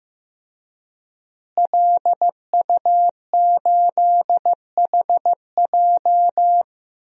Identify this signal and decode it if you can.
Morse code, heard as a keyed tone, about 15 wpm, 700 Hz. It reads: LU8HJ
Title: DXCC-100 Call Signs - 5 Characters - Repeated 3x - 15wpm